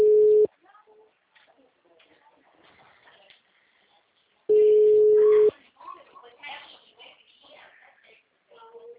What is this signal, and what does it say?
Ringback tone (Europe, 425 Hz), with speech.
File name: out- -2007- - - .23567